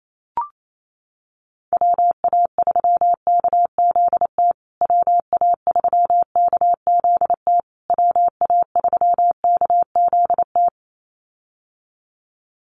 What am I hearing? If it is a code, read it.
Morse code, heard as a keyed tone, about 28 wpm, 700 Hz. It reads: WA3KZT WA3KZT WA3KZT
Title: USA Call Signs Repeated 3x 28wpm